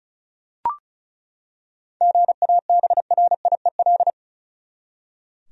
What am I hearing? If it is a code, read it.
Morse code, heard as a keyed tone, about 34 wpm, 700 Hz. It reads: GABRIEL
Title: USA Common Names - 35wpm